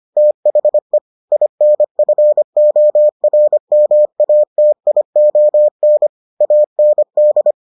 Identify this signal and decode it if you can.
Morse code, heard as a keyed tone, about 25 wpm, 600 Hz. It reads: THE INFORMATION AND